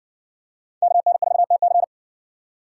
Morse code, heard as a keyed tone, about 55 wpm, 700 Hz. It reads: KN4TC